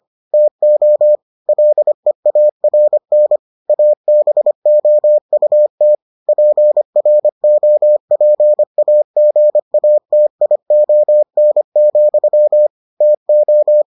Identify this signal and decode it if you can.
Morse code, heard as a keyed tone, about 25 wpm, 600 Hz. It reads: TO LEARN ABOUT PROPAGATION, TO